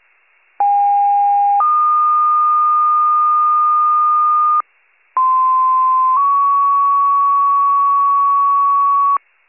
Two-tone page, heard as a one-second tone, then a three-second tone, about 810 then 1220 Hz, about 1040 then 1120 Hz. It is heard from 0.6 to 9.2 s.
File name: pager / 95 RESCUE / wcalert